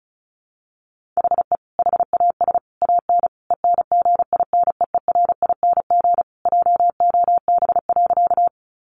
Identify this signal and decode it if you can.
Morse code, heard as a keyed tone, about 34 wpm, 700 Hz. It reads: HE HAS AN ENGINEERING JOB.
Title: Sentences from Top 1500 Words 35wpm